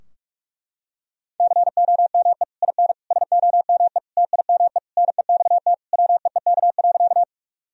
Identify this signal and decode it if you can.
Morse code, heard as a keyed tone, about 46 wpm, 700 Hz. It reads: COME IN SOME TIME NEXT WEEK.
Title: Sentences from Top 1000 Words - Review Part 2 - 45wpm